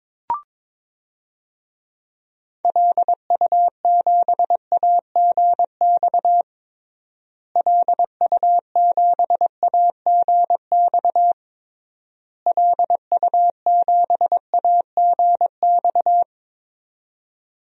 Morse code, heard as a keyed tone, about 22 wpm, 700 Hz. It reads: LU7AGX LU7AGX LU7AGX
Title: DXCC-100 Call Signs - 6 Characters - Repeated 3x - 22wpm